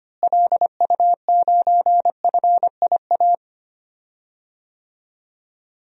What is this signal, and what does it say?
Morse code, heard as a keyed tone, about 25 wpm, 700 Hz. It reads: LU9FIA
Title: DXCC-100 Call Signs - 6 Characters - Repeated 3x - 25wpm